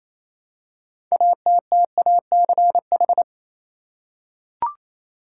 Morse code, heard as a keyed tone, about 28 wpm, 700 Hz. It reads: ATTACH
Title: Six Letter Words - 28wpm